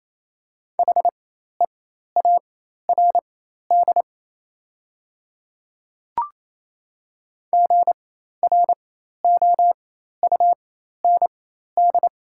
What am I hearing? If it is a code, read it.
Morse code, heard as a keyed tone, about 28 wpm, spaced as for 13 wpm, 700 Hz. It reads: HEARD GROUND
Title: 4x Character Spacing - Top 500 Words - 28wpm